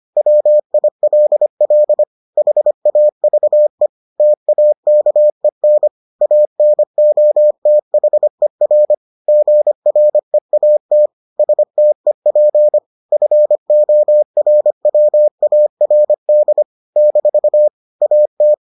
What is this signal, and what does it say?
Morse code, heard as a keyed tone, about 25 wpm, 600 Hz. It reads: WILL HAVE TAKEN ANOTHER GREAT STEP FORWARD - AT